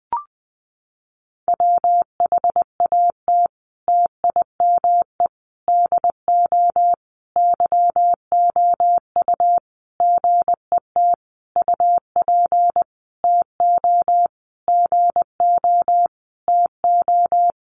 Morse code, heard as a keyed tone, about 20 wpm, 700 Hz. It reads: WHAT TIME DO YOU GET UP TO GO TO
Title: Sentences from Top 1500 Words 20wpm